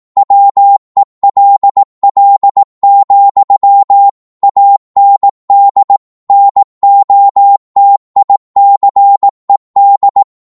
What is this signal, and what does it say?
Morse code, heard as a keyed tone, about 18 wpm, 800 Hz. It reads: WELL, AND NOTICED
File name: Morse / psg-18wpm